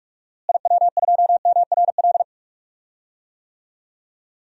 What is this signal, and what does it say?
Morse code, heard as a keyed tone, about 46 wpm, 700 Hz. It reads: IW2MRL
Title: DXCC-100 Call Signs - 6 Characters - Repeated 3x - 45wpm